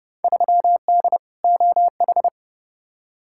Morse code, heard as a keyed tone, about 30 wpm, 700 Hz. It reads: 3D OH
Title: Field Day Exchanges - 30wpm